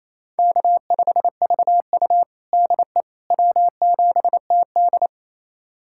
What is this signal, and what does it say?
Morse code, heard as a keyed tone, about 28 wpm, 700 Hz. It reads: K5VU DE W7TD
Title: USA Callsign DE Callsign - 4 Characters - 28wpm